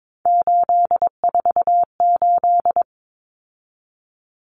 Morse code, heard as a keyed tone, about 22 wpm, 700 Hz. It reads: 848
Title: Numbers - 3-Digit 22wpm